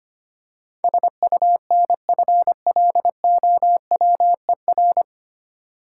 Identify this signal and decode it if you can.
Morse code, heard as a keyed tone, about 25 wpm, 700 Hz. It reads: SUNFLOWER
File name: practice / 25wpm / Difficult Words with Hints - 25wpm